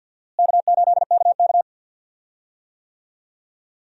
Morse code, heard as a keyed tone, about 52 wpm, 700 Hz. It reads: K8KK
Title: USA Call Signs - 4 Characters - Repeated 2x - 50wpm